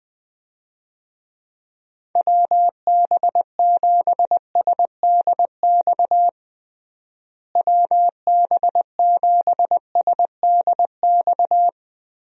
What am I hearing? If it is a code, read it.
Morse code, heard as a keyed tone, about 20 wpm, 700 Hz. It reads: WB7SDX WB7SDX
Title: USA Call Signs - 6 Characters - Repeated 3x - 20wpm